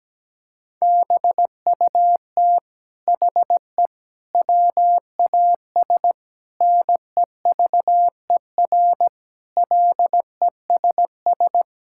Morse code, heard as a keyed tone, about 17 wpm, 700 Hz. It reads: BUT HE WAS NEVER LESS
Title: Sentences from Top 1000 Words - Review Part 5 - 17wpm